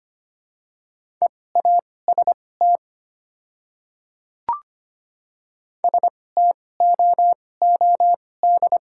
Morse code, heard as a keyed tone, about 25 wpm, spaced as for 18 wpm, 700 Hz. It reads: EAST STOOD
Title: 2x Character Spacing - Top 500 Words - 25wpm